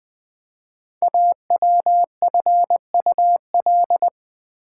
Morse code, heard as a keyed tone, about 20 wpm, 700 Hz. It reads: AWFUL
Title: 5 Letter Words - First 3 Letters Given - 20wpm